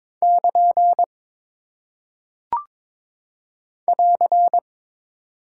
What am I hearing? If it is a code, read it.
Morse code, heard as a keyed tone, about 22 wpm, 700 Hz. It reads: <KN> <AR>
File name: practice / 22wpm / Single Punctuation-Prosign - 22wpm